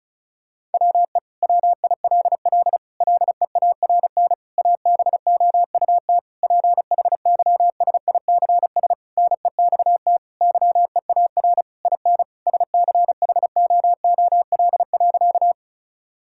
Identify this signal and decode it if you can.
Morse code, heard as a keyed tone, about 36 wpm, 700 Hz. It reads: WE WILL LEARN ABOUT PHYSICS NEXT YEAR IN SCHOOL.
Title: Sentences from Top 1000 Words - Review Part 5 - 35wpm